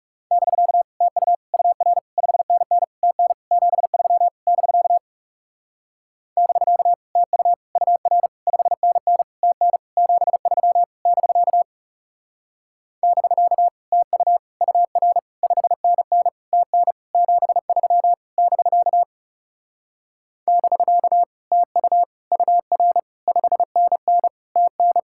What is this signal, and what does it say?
Morse code, heard as a keyed tone, about 37 wpm, 700 Hz. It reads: <BK> TU UR 5NN TN 73 <BK> <BK> TU UR 5NN TN 73 <BK> <BK> TU UR 5NN TN 73 <BK> <BK> TU UR 5NN TN
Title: Speed-Racing Common QSO Phrases - 30wpm